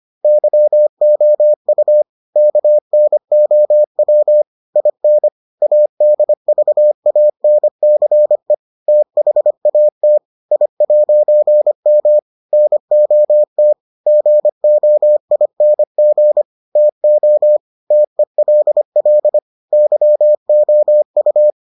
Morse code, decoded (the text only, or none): YOU KNOW IN ADVANCE THAT I'M NOT GOING TO TELL YOU